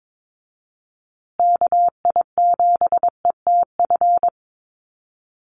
Morse code, heard as a keyed tone, about 22 wpm, 700 Hz. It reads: KI7ETF
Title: USA Call Signs Repeated 3x 22wpm